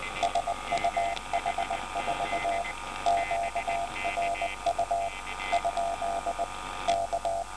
Morse code, heard as a keyed tone, about 20 wpm, 690 Hz, with noise. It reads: SUH4 QRU ? K